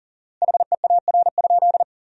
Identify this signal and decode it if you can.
Morse code, heard as a keyed tone, about 40 wpm, 700 Hz. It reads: HEAR?
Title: 4x Word-Spacing - Sentences from Top 300 Words - 40wpm